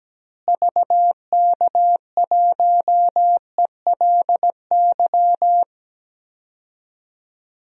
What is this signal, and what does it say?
Morse code, heard as a keyed tone, about 17 wpm, 700 Hz. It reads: VK1ELY
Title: DXCC-100 Call Signs - 6 Characters - Repeated 3x - 17wpm